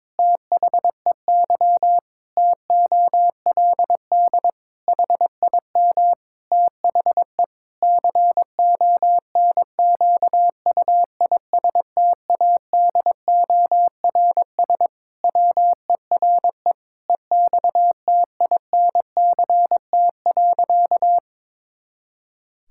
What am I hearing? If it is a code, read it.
Morse code, heard as a keyed tone, about 22 wpm, 700 Hz. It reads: THEY TOLD HIM THE CONQUISTADORS WERE EXTINCT.
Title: Pangram Challenge v2 - 22wpm